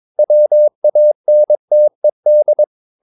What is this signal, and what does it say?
Morse code, heard as a keyed tone, about 22 wpm, 600 Hz. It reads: WANTED
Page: Romanen Never af Ken Follet på engelsk.